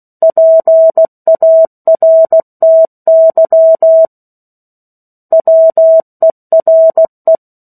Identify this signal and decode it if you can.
Morse code, heard as a keyed tone, about 16 wpm, 650 Hz. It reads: PARTY WERE
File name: 03-Chapter-03 16 WPM